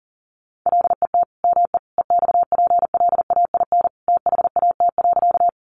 Morse code, heard as a keyed tone, about 40 wpm, 700 Hz. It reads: LET ME EXPLAIN THAT.
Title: Sentences from Top 1100 Words 40wpm